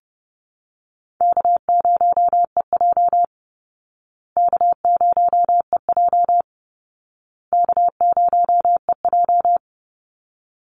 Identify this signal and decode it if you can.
Morse code, heard as a keyed tone, about 29 wpm, 700 Hz. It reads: K0EJ K0EJ K0EJ
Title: Easy Call Signs Repeated 3x - 30wpm